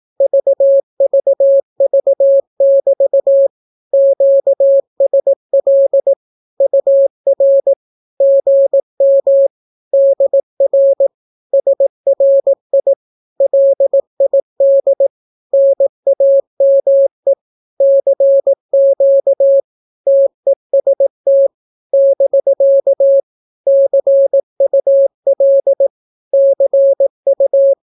Morse code, heard as a keyed tone, about 18 wpm, 550 Hz. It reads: VVV<BT> QSL UR GM DR SRI LID NAME CQ TEST <BK> CUL CU
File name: naukapodstaw / skroty / skroty